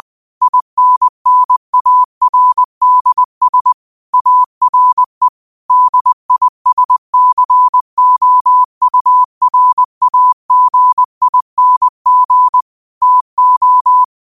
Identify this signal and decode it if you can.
Morse code, heard as a keyed tone, about 20 wpm, 1000 Hz. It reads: INNARDS ARE DISCOURAGING TO
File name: qotdfiles / 20wpm